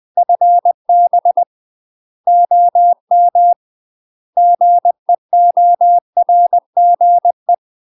Morse code, heard as a keyed tone, about 20 wpm, 700 Hz. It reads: FB OM GEORGE
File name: UK MASTER QSOs TRAINING / 20WPM / UK MASTER QSOs 0055